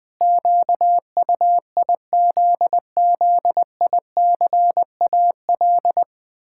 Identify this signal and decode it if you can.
Morse code, heard as a keyed tone, about 21 wpm, 700 Hz. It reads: QUIZZICAL
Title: Pangram Challenge v2 - 20wpm